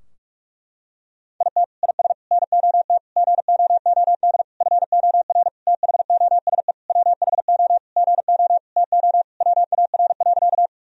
Morse code, heard as a keyed tone, about 43 wpm, 700 Hz. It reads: IT IS NOT GOOD FOR THOSE WHO GO TO WAR.
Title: Sentences from Top 1000 Words - Review Part 2 - 45wpm